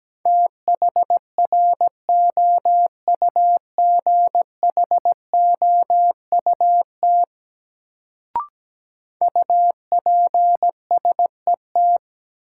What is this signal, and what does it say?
Morse code, heard as a keyed tone, about 17 wpm, 700 Hz. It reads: THROUGHOUT UPSET
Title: Sentences from Top 800 Words - New Word Drill - 17wpm